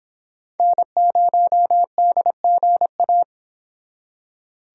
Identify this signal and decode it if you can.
Morse code, heard as a keyed tone, about 30 wpm, 700 Hz. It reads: N0DGA